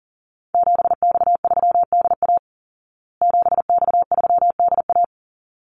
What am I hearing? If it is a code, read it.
Morse code, heard as a keyed tone, about 40 wpm, 700 Hz. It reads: 7X3DA 7X3DA
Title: Call Signs Repeated 3x - 40wpm